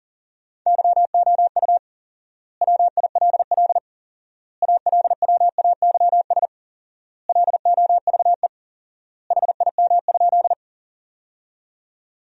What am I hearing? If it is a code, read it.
Morse code, heard as a keyed tone, about 40 wpm, 700 Hz. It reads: YOU WILL ALWAYS LOVE HIM?
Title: 4x Word-Spacing - Sentences from Top 400 Words - 40wpm